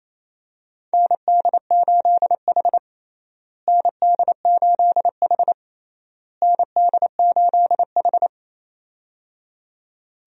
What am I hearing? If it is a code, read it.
Morse code, heard as a keyed tone, about 28 wpm, 700 Hz. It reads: ND8H ND8H ND8H
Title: USA Call Signs - 4 Characters - Repeated 3x - 28wpm